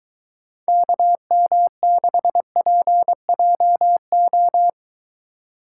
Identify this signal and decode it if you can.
Morse code, heard as a keyed tone, about 23 wpm, 700 Hz. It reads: KM6PJO